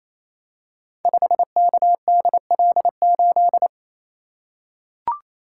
Morse code, heard as a keyed tone, about 27 wpm, 700 Hz. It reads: 5KDL8
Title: Code Groups - 5 Letter:Numbers - 28wpm